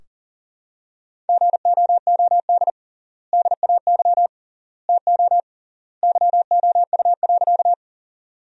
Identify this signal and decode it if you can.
Morse code, heard as a keyed tone, about 40 wpm, 700 Hz. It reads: GOOD DAY TO YOU.
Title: 3x Word-Spacing - Sentences from Top 200 Words - 40wpm